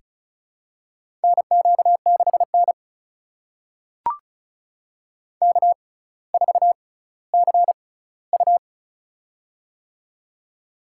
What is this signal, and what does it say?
Morse code, heard as a keyed tone, about 34 wpm, spaced as for 18 wpm, 700 Hz. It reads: NQ6N K4CU
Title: USA Call Signs - 4 Characters - 6x Character Spacing - 35wpm